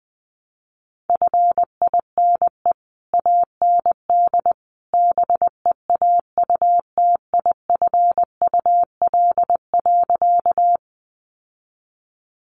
Morse code, heard as a keyed tone, about 20 wpm, 700 Hz. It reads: FINE AND BEAUTIFUL.
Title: Sentences from Top 900 Words 20wpm